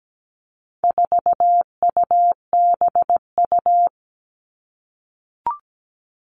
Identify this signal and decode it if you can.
Morse code, heard as a keyed tone, about 17 wpm, 700 Hz. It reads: 4UBU
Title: All Cut Numbers - 4-Digit 17wpm